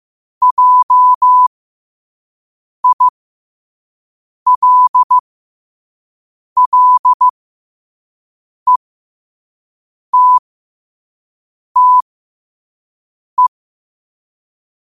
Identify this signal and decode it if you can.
Morse code, heard as a keyed tone, about 15 wpm, spaced as for 5 wpm, 1000 Hz. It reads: JILLETTE
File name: qotdfiles / 5wpm